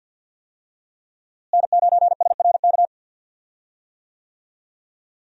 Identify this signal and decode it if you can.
Morse code, heard as a keyed tone, about 51 wpm, 700 Hz. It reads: N9SRK
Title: USA Call Signs - 5 Characters - Repeated 2x - 50wpm